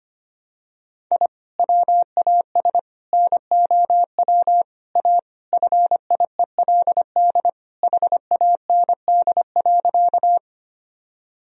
Morse code, heard as a keyed tone, about 25 wpm, 700 Hz. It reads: I WAS NOW A FIELD HAND.
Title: Sentences from Top 1000 Words - Review Part 2 - 25wpm